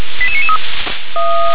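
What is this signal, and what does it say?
Five-tone selective call (ZVEI), heard as quick runs of five tones, tones of about 70 ms. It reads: 88993
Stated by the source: ZVEI five-tone selective call received by SDR.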